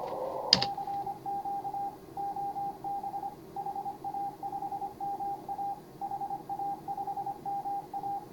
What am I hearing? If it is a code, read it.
Morse code, heard as a keyed tone, about 24 wpm, 810 Hz. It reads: CQ OB HA5MA HA5MA
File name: CW / HA5MA RHOB HANG CW 2017